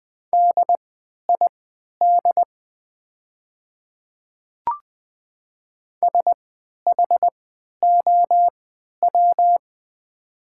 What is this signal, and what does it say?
Morse code, heard as a keyed tone, about 20 wpm, spaced as for 11 wpm, 700 Hz. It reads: DID SHOW
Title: 3x Character Spacing - Top 200 Words - 20wpm